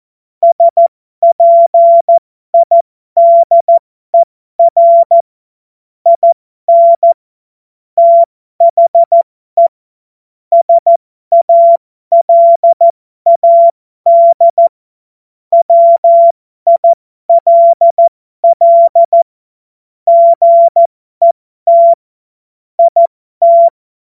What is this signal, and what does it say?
Morse code, heard as a keyed tone, about 14 wpm, spaced as for 12 wpm, 680 Hz. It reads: SPIDER IN THE SALAD WILL GET IT
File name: Morsekurs / Witze